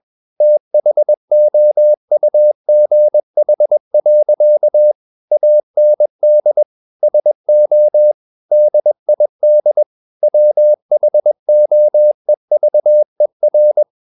Morse code, heard as a keyed tone, about 21 wpm, 600 Hz. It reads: THOUGH. AND SO DID WHOEVER
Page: CW med Gnister